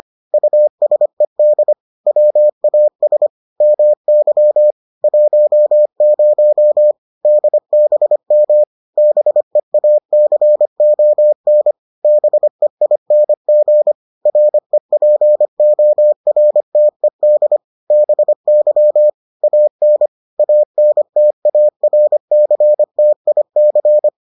Morse code, heard as a keyed tone, about 25 wpm, 600 Hz. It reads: USED WAS MY 10 DBM BEACON BEING REPORTED BY AN ANTARCTIC